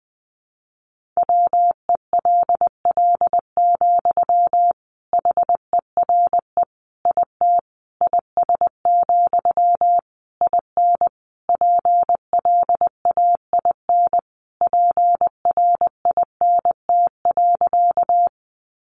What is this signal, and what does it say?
Morse code, heard as a keyed tone, about 20 wpm, 700 Hz. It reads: WELL, HERE IT IS, IN PLAIN PRINT.
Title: Sentences from Top 1400 Words 20wpm